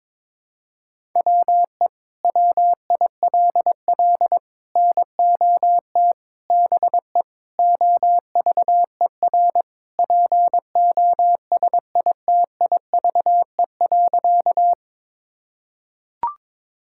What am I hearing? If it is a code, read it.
Morse code, heard as a keyed tone, about 22 wpm, 700 Hz. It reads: WE WILL NOT BE OVER POSITIVE.
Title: Sentences from Top 1000 Words - Review Part 5 - 22wpm